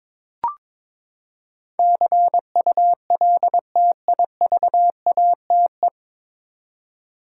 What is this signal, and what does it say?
Morse code, heard as a keyed tone, about 22 wpm, 700 Hz. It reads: CULTIVATE
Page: Morse Code Ninja practice 22wpm